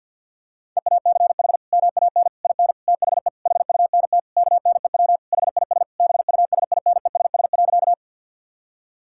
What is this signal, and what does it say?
Morse code, heard as a keyed tone, about 50 wpm, 700 Hz. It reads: EACH MAN IN THE HUNT KNEW HIS BUSINESS.
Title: Sentences from Top 1600 Words 50wpm